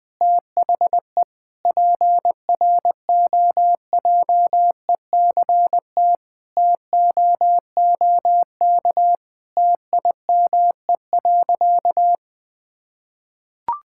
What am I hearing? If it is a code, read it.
Morse code, heard as a keyed tone, about 20 wpm, 700 Hz. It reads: THE PROJECT TOOK TIME.
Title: Sentences with 3-Word Phrases - 20wpm